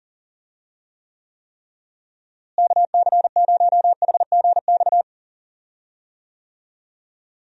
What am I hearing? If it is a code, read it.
Morse code, heard as a keyed tone, about 40 wpm, 700 Hz. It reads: KC0HGX
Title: USA Call Signs - 6 Characters - Repeated 3x - 40wpm